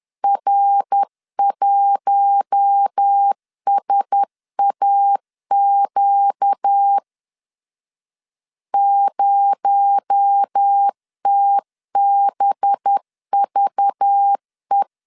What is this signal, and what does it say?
Morse code, heard as a keyed tone, about 10 wpm, 800 Hz. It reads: R1SAQ 0TBVE
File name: LEZIONI / 6 PKX0 / 6 PKX0 10wpm 2707